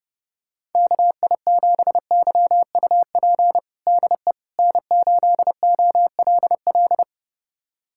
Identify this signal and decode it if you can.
Morse code, heard as a keyed tone, about 32 wpm, 700 Hz. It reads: KI7YUP DE N8OLL
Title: USA Callsign DE Callsign 30wpm